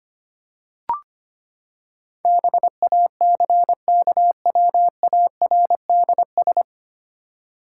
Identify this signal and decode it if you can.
Morse code, heard as a keyed tone, about 25 wpm, 700 Hz. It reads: BACKWARDS
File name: Nine Letter Words - 25wpm